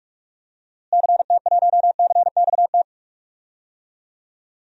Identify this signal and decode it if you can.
Morse code, heard as a keyed tone, about 42 wpm, 700 Hz. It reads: CT1CXT
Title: DXCC-100 Call Signs - 6 Characters - Repeated 3x - 45wpm